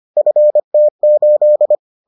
Morse code, heard as a keyed tone, about 25 wpm, 600 Hz. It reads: FT8